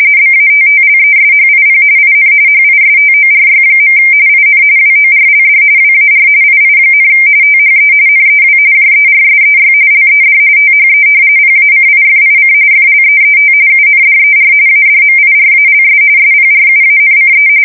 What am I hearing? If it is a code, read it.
RTTY, about 45.45 baud. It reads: BATTLESHIP IN HISTORY. IN WORLD WAR II, THE NEW JERSEY LED THE PACIFIC FLEET UNDER ADMIRALS SPRUANCE
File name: fsk170 / battleshipnj